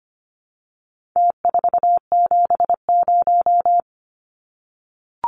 Morse code, heard as a keyed tone, about 25 wpm, 700 Hz. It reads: T470
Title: Numbers - Contest Style - 4-Digit 25wpm